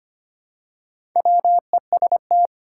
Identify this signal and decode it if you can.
Morse code, heard as a keyed tone, about 25 wpm, 700 Hz. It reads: WEST
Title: Pangram Challenge v2 - 25wpm